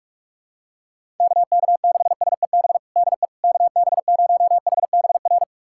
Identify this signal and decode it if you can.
Morse code, heard as a keyed tone, about 46 wpm, 700 Hz. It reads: KK6SEB DE KB0HBR
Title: USA Callsign DE Callsign - 6 Characters - 45wpm